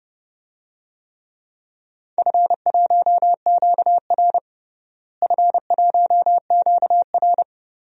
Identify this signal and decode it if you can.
Morse code, heard as a keyed tone, about 29 wpm, 700 Hz. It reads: F1QR F1QR
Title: DXCC-100 Call Signs - 4 Characters - Repeated 3x - 30wpm